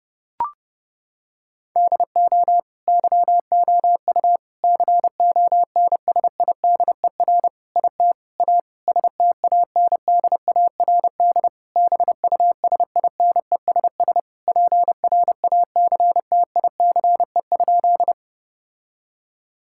Morse code, heard as a keyed tone, about 30 wpm, 700 Hz. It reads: DO YOU CONSIDER IT A STANDARD BUSINESS PRACTICE?
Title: Sentences from Top 1000 Words - Review Part 6 - 30wpm